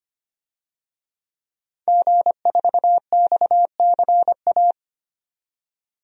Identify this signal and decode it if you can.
Morse code, heard as a keyed tone, about 25 wpm, 700 Hz. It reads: G4XCA
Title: DXCC-100 Call Signs - 5 Characters - 25wpm